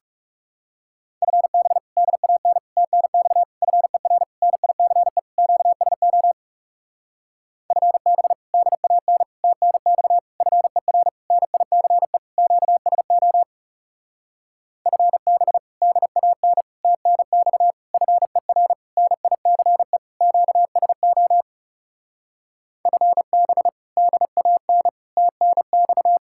Morse code, heard as a keyed tone, about 39 wpm, 700 Hz. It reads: FB DAN TNX FER NICE QSO FB DAN TNX FER NICE QSO FB DAN TNX FER NICE QSO FB DAN TNX